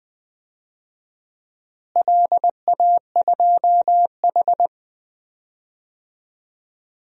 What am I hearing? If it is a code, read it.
Morse code, heard as a keyed tone, about 20 wpm, 700 Hz. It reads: LA2H
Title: DXCC-100 Call Signs - 4 Characters - Repeated 2x - 20wpm